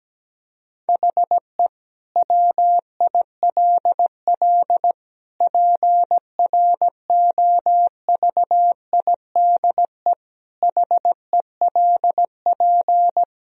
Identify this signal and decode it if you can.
Morse code, heard as a keyed tone, about 17 wpm, 700 Hz. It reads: HE WILL PROVIDE HELP
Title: Sentences from Top 1000 Words - Review Part 3 - 17wpm